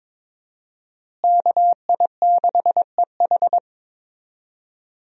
Morse code, heard as a keyed tone, about 21 wpm, 700 Hz. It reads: KI6EH
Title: USA Call Signs - 5 Characters - Repeated 3x - 22wpm